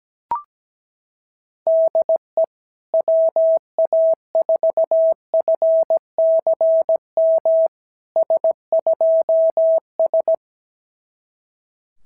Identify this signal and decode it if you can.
Morse code, heard as a keyed tone, about 17 wpm, 650 Hz. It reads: DE WA4FCM S2S
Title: SOTA QSO Practice 17wpm